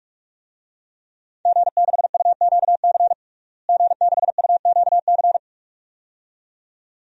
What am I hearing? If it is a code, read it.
Morse code, heard as a keyed tone, about 42 wpm, 700 Hz. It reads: G6UQC G6UQC
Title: DXCC-100 Call Signs - 5 Characters - Repeated 2x - 45wpm